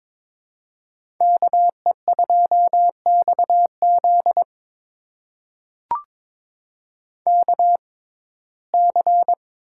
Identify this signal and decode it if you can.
Morse code, heard as a keyed tone, about 22 wpm, spaced as for 16 wpm, 700 Hz. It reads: KE2XZ KC
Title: USA Call Signs - 6x Character Spacing - 22wpm